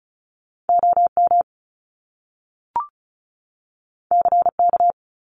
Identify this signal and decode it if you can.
Morse code, heard as a keyed tone, about 36 wpm, 700 Hz. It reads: OM CK
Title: Top Two Letter Combinations in Words 35wpm